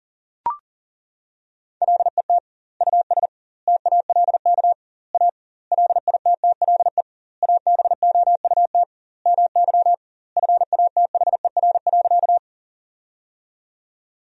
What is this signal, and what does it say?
Morse code, heard as a keyed tone, about 40 wpm, 700 Hz. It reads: LET US TALK A LITTLE ABOUT MY FATHER.
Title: 2x Word-Spacing - Sentences from Top 400 Words - 40wpm